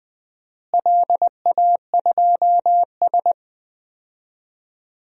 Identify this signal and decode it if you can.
Morse code, heard as a keyed tone, about 21 wpm, 700 Hz. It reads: LA2S
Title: DXCC-100 Call Signs - 4 Characters - Repeated 2x - 20wpm